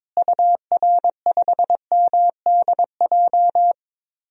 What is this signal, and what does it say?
Morse code, heard as a keyed tone, about 21 wpm, 700 Hz. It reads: UR5MDJ
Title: DXCC-100 Call Signs - 6 Characters - Repeated 3x - 22wpm